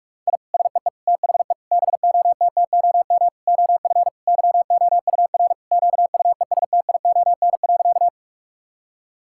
Morse code, heard as a keyed tone, about 46 wpm, 700 Hz. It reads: I SEE THE BOTTOM OF YOUR QUESTION.
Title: Sentences from Top 1000 Words - Review Part 6 - 45wpm